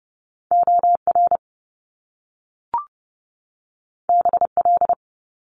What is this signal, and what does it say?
Morse code, heard as a keyed tone, about 30 wpm, 700 Hz. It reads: OR BL
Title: Top Two Letter Combinations in Words 30wpm